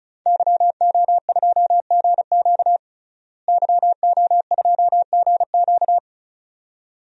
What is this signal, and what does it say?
Morse code, heard as a keyed tone, about 34 wpm, 700 Hz. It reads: YO2GQ YO2GQ
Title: DXCC-100 Call Signs - 5 Characters - Repeated 2x - 35wpm